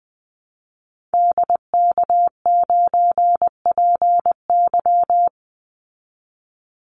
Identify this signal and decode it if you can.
Morse code, heard as a keyed tone, about 20 wpm, 700 Hz. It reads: DK9PY
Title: Call Signs Repeated 2x - 20wpm